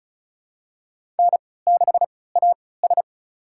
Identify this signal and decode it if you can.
Morse code, heard as a keyed tone, about 36 wpm, spaced as for 20 wpm, 700 Hz. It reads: N6AS
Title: USA Call Signs - 4 Characters - 3x Character Spacing - 35wpm